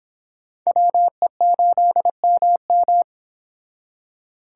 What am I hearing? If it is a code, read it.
Morse code, heard as a keyed tone, about 29 wpm, 700 Hz. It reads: WE8MM